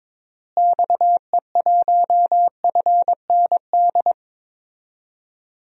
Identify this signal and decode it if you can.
Morse code, heard as a keyed tone, about 21 wpm, 700 Hz. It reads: XE1FND
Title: DXCC-100 Call Signs - 6 Characters - Repeated 3x - 22wpm